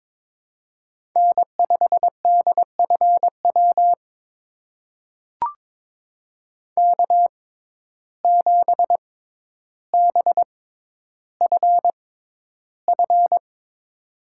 Morse code, heard as a keyed tone, about 22 wpm, spaced as for 11 wpm, 700 Hz. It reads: N5DFW K7BFF
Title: USA Call Signs - 5 Characters - 6x Character Spacing - 22wpm